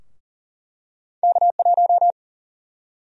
Morse code, heard as a keyed tone, about 40 wpm, 700 Hz. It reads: K1
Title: USA Call Sign Prefixes - 40wpm